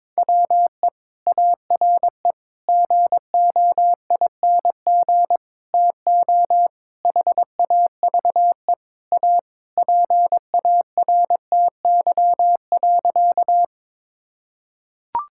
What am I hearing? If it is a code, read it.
Morse code, heard as a keyed tone, about 22 wpm, 700 Hz. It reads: WE ARE GOING TO HAVE A PARTY.
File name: practice / 22wpm / Sentences with 4-Word Phrases - 22wpm